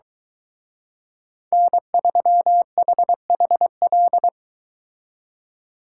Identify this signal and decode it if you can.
Morse code, heard as a keyed tone, about 23 wpm, 700 Hz. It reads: N3HHL